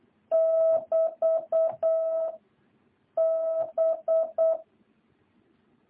Morse code, heard as a keyed tone, about 8 wpm, 650 Hz. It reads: <BT> B